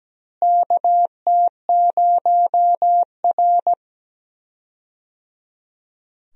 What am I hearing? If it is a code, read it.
Morse code, heard as a keyed tone, about 17 wpm, 700 Hz. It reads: KT0R
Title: USA Call Signs - 4 Characters - Repeated 3x - 17wpm